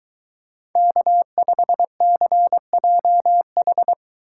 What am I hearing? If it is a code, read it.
Morse code, heard as a keyed tone, about 23 wpm, 700 Hz. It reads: K5CJH